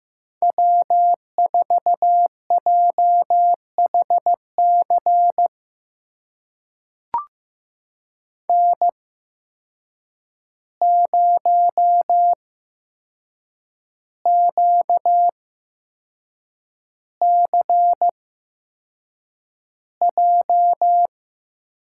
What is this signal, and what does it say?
Morse code, heard as a keyed tone, about 15 wpm, spaced as for 6 wpm, 700 Hz. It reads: W4JHC N0QCJ
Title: USA Call Signs - 5 Characters - 8x Character Spacing - 15wpm